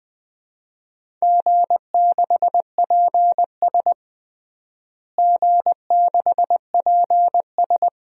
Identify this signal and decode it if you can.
Morse code, heard as a keyed tone, about 20 wpm, 700 Hz. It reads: G6PS G6PS
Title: DXCC-100 Call Signs - 4 Characters - Repeated 2x - 20wpm